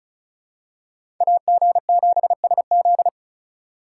Morse code, heard as a keyed tone, about 36 wpm, 700 Hz. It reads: AG7SZ
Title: USA Call Signs - 6x Character Spacing - 35wpm